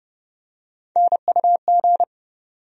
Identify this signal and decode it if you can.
Morse code, heard as a keyed tone, about 30 wpm, 700 Hz. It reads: NUG
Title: USA Call Sign Postfixes - 3 Letters 30wpm